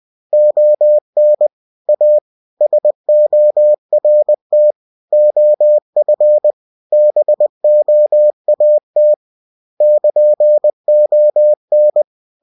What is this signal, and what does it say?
Morse code, heard as a keyed tone, about 20 wpm, 600 Hz. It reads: ON A SORT OF BOAT <KN>ON
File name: cw-books / winnie-the-pooh / winnie-20wpm-page-0104